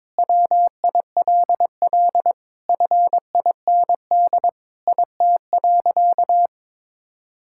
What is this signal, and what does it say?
Morse code, heard as a keyed tone, about 22 wpm, 700 Hz. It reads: WILL FIND IT.
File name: practice / 22wpm / Sentences with 4-Word Phrases - 22wpm